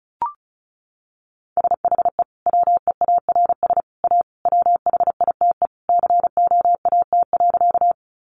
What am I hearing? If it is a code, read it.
Morse code, heard as a keyed tone, about 34 wpm, 700 Hz. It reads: SHE WEARS A WHITE COAT.
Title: Additional Easy Phrases 35wpm